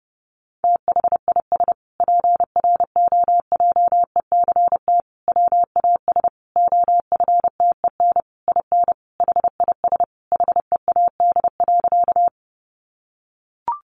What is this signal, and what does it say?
Morse code, heard as a keyed tone, about 30 wpm, 700 Hz. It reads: THIS PROJECT WAS OFTEN IN HIS HEAD.
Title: Sentences from Top 1400 Words 30wpm